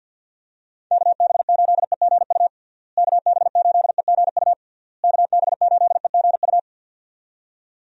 Morse code, heard as a keyed tone, about 50 wpm, 700 Hz. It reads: KB8EGU KB8EGU KB8EGU
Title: USA Call Signs - 6 Characters - Repeated 3x - 50wpm